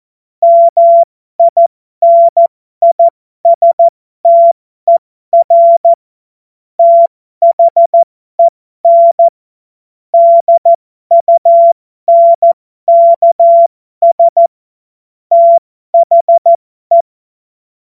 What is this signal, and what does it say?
Morse code, heard as a keyed tone, about 14 wpm, spaced as for 12 wpm, 680 Hz. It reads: MINISTER THEN DUNKS THE